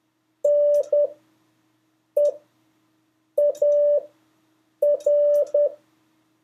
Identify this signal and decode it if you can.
Morse code, heard as a keyed tone, about 10 wpm, spaced as for 6 wpm, 580 Hz. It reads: NEAR
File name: 2of5-words-1